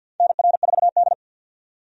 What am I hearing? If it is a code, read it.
Morse code, heard as a keyed tone, about 47 wpm, 700 Hz. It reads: NR4D
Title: USA Call Signs - 4 Characters - 50wpm